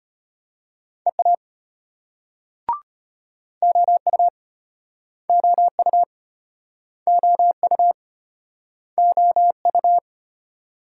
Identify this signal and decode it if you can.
Morse code, heard as a keyed tone, about 36 wpm, 700 Hz. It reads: EA OU OU OU OU